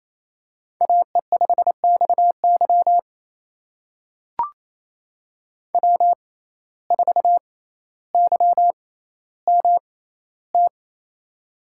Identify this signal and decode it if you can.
Morse code, heard as a keyed tone, about 28 wpm, spaced as for 14 wpm, 700 Hz. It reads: AE5XY W4YMT
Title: USA Call Signs - 5 Characters - 6x Character Spacing - 28wpm